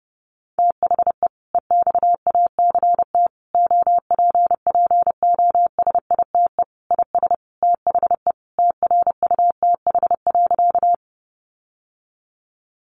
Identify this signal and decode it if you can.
Morse code, heard as a keyed tone, about 30 wpm, 700 Hz. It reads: THE EXACT OPPOSITE IS THE TRUTH.
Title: Sentences from Top 1300 Words 30wpm